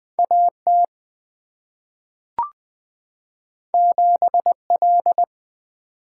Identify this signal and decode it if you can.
Morse code, heard as a keyed tone, about 20 wpm, 700 Hz. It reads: A T 7 L
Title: Two Letter-Number Combinations - 20wpm